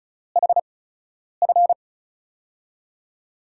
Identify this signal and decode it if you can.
Morse code, heard as a keyed tone, about 36 wpm, spaced as for 10 wpm, 700 Hz. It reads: HF
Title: USA Call Signs - 5 Characters - 8x Character Spacing - 35wpm